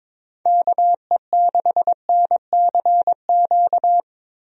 Morse code, heard as a keyed tone, about 22 wpm, 700 Hz. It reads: KE6NCQ
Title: USA Call Signs - 6 Characters - Repeated 2x - 22wpm